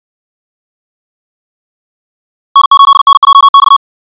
Morse code, heard as a keyed tone, about 56 wpm, 1110 Hz. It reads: N8NJX
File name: ja7uhv / SampleHST2